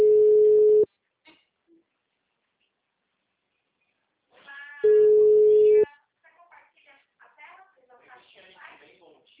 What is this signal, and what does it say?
Ringback tone (Europe, 425 Hz), with speech.